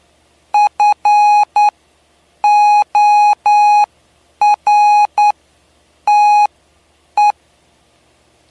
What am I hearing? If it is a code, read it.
Morse code, heard as a keyed tone, about 10 wpm, spaced as for 7 wpm, 830 Hz. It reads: FORTE
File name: cw / py2lcd / 07-PY2LCD-Li��o 7